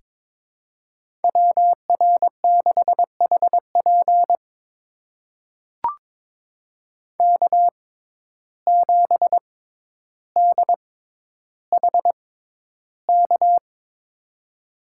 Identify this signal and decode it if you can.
Morse code, heard as a keyed tone, about 22 wpm, spaced as for 11 wpm, 700 Hz. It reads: WR6HP K7DHK
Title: USA Call Signs - 5 Characters - 6x Character Spacing - 22wpm